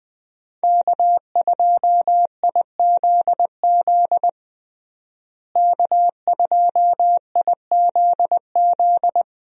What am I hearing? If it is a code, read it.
Morse code, heard as a keyed tone, about 20 wpm, 700 Hz. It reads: K2IZZ K2IZZ
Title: USA Call Signs - 5 Characters - Repeated 2x - 20wpm